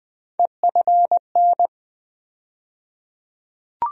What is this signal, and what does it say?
Morse code, heard as a keyed tone, about 20 wpm, 700 Hz. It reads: EFN
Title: USA Call Sign Postfixes - 3 Letters 20wpm